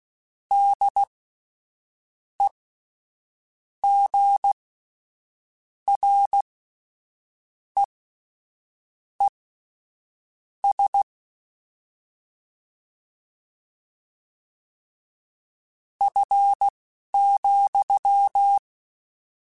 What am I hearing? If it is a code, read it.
Morse code, heard as a keyed tone, about 16 wpm, spaced as for 6 wpm, 800 Hz. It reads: DEGREES F,
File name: Spaced Characters / Spaced Characters-mp3s / Weather 5 1 sp @10